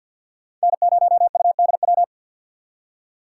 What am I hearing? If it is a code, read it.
Morse code, heard as a keyed tone, about 48 wpm, 700 Hz. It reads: N0UDW
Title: USA Call Signs - 5 Characters - 2x Character Spacing - 50wpm